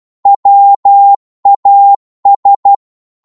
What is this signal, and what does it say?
Morse code, heard as a keyed tone, about 12 wpm, 800 Hz. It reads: WAS